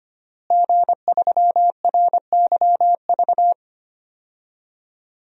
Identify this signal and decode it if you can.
Morse code, heard as a keyed tone, about 25 wpm, 700 Hz. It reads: G3RYV
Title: DXCC-100 Call Signs - 5 Characters - Repeated 2x - 25wpm